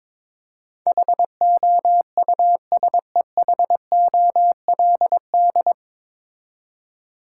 Morse code, heard as a keyed tone, about 22 wpm, 700 Hz. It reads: HOUSEHOLD
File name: 9 Letter Words - First 3 Letters Given - 22wpm